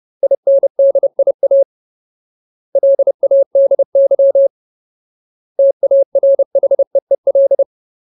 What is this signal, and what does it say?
Morse code, heard as a keyed tone, about 30 wpm, 550 Hz. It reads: INDIA LADY TARHEEL
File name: Ultimate Word List / 30WPM Ultimate